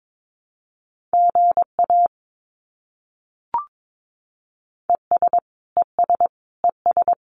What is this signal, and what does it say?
Morse code, heard as a keyed tone, about 22 wpm, 700 Hz. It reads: GA ES ES ES
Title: Abbreviations Repeated 3x - 22wpm